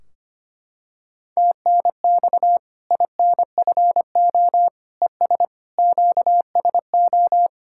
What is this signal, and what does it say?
Morse code, heard as a keyed tone, about 25 wpm, 700 Hz. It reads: TNX INFO ES QSO
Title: Practice QSO - Exchange 4 - Ending - 25wpm